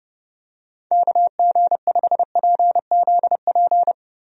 Morse code, heard as a keyed tone, about 29 wpm, 700 Hz. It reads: KG5PZP
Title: USA Call Signs - 6 Characters - Repeated 3x - 30wpm